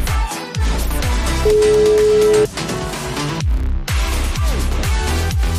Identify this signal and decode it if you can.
Ringback tone (Europe, 425 Hz), with noise.